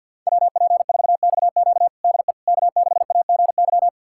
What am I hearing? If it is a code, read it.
Morse code, heard as a keyed tone, about 52 wpm, 700 Hz. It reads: WP4XQ DE K6AGY
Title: USA Callsign DE Callsign 50wpm